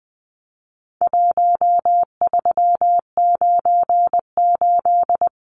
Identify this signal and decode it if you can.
Morse code, heard as a keyed tone, about 20 wpm, 700 Hz. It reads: 1398
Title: Numbers - Contest Style - 4-Digit 20wpm